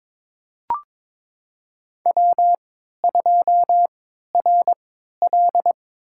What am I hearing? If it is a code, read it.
Morse code, heard as a keyed tone, about 22 wpm, spaced as for 13 wpm, 700 Hz. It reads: W2RL
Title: USA Call Signs - 3x Character Spacing - 22wpm